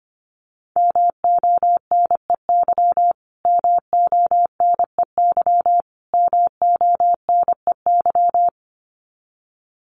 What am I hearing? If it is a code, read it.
Morse code, heard as a keyed tone, about 25 wpm, 700 Hz. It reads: MONEY MONEY MONEY
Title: Top 500 Words Repeated 25wpm